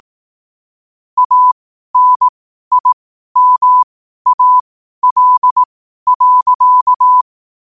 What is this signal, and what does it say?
Morse code, heard as a keyed tone, about 18 wpm, spaced as for 12 wpm, 1000 Hz. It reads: ANIMAL.
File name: qotdfiles / 12wpm